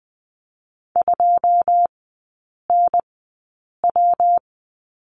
Morse code, heard as a keyed tone, about 20 wpm, 700 Hz. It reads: 2 N W
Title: 133 Lesson 22e - Question Mark - Three Characters 20wpm